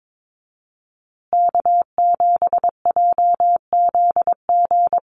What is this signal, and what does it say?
Morse code, heard as a keyed tone, about 22 wpm, 700 Hz. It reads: K7JZG
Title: USA Call Signs Repeated 3x 22wpm